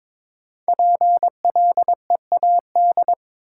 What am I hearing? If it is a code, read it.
Morse code, heard as a keyed tone, about 22 wpm, 700 Hz. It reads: PLEAD
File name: 5 Letter Words - First 3 Letters Given - 22wpm